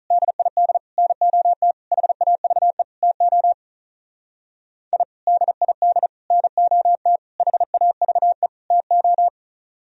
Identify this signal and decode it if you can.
Morse code, heard as a keyed tone, about 44 wpm, 700 Hz. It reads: DID NOT HAVE TO I DID NOT HAVE TO